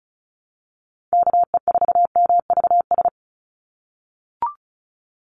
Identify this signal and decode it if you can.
Morse code, heard as a keyed tone, about 34 wpm, 700 Hz. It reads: KE4MVS
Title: USA Call Signs 35wpm